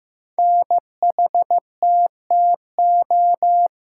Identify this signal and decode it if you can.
Morse code, heard as a keyed tone, about 15 wpm, 700 Hz. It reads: NHTTO
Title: Code Groups - 5 Letters - 15wpm